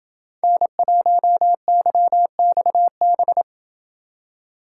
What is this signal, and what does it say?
Morse code, heard as a keyed tone, about 29 wpm, 700 Hz. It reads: N1YXB